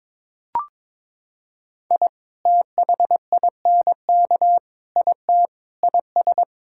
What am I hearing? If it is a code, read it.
Morse code, heard as a keyed tone, about 22 wpm, 700 Hz. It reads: I THINK IT IS
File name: practice / 22wpm / Sentences with 3-Word Phrases - 22wpm